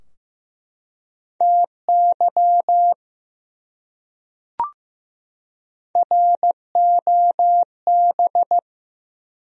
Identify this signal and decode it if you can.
Morse code, heard as a keyed tone, about 15 wpm, 700 Hz. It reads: TY ROB
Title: USA Common Names - Max 3 Letters - 15wpm